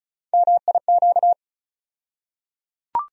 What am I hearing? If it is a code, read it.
Morse code, heard as a keyed tone, about 34 wpm, 700 Hz. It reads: MIQ
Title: USA Call Sign Postfixes - 3 Letters 35wpm